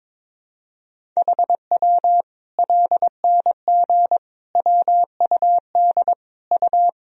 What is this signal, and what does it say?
Morse code, heard as a keyed tone, about 22 wpm, 700 Hz. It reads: HW LNG WUD U
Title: Sentences from Top 500 Words and Abbreviations 22wpm